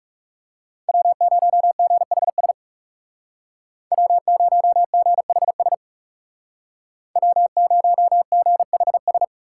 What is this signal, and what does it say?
Morse code, heard as a keyed tone, about 37 wpm, 700 Hz. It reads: W0GHS W0GHS W0GHS
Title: Speed Racing - USA Call Signs 30wpm